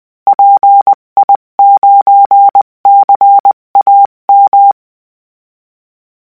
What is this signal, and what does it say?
Morse code, heard as a keyed tone, about 20 wpm, 800 Hz. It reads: PI9CAM
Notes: a ‘walkie-talkie’ at 145.800 MHz suffices